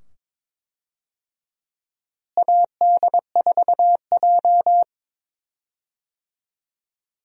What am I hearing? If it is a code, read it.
Morse code, heard as a keyed tone, about 21 wpm, 700 Hz. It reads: AD4J
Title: USA Call Signs - 4 Characters - Repeated 2x - 22wpm